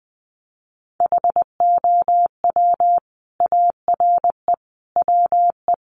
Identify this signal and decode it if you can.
Morse code, heard as a keyed tone, about 20 wpm, 700 Hz. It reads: HOW ARE WE
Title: Sets of 5 Words Encore 20wpm